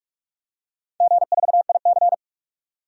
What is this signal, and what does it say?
Morse code, heard as a keyed tone, about 44 wpm, 700 Hz. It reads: G4IC
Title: DXCC-100 Call Signs - 4 Characters - 45wpm